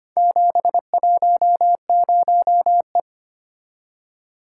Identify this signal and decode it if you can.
Morse code, heard as a keyed tone, about 25 wpm, 700 Hz. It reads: 710E
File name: practice / 25wpm / Four Letter-Number Combinations - 25wpm